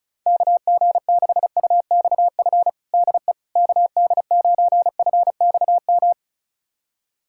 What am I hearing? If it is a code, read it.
Morse code, heard as a keyed tone, about 34 wpm, 700 Hz. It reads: KG6UXF DE KD9FXM
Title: USA Callsign DE Callsign 35wpm